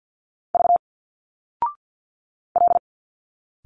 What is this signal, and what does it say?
Morse code, heard as a keyed tone, about 52 wpm, 700 Hz. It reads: V L
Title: Single Letters 50wpm